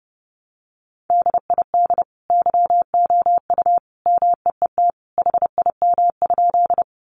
Morse code, heard as a keyed tone, about 30 wpm, 700 Hz. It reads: DID YOU MEET HIM?
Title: Sentences from Top 1100 Words 30wpm